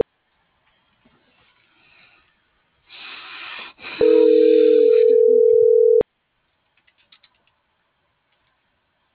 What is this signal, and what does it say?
Ringback tone (North America), 440 + 480 Hz.
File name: CENTRAL MAYAB / out vianey.prado